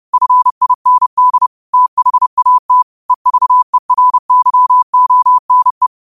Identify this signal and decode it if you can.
Morse code, heard as a keyed tone, about 30 wpm, 1000 Hz. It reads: FIND THAT EVERYONE
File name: qotdfiles / 30wpm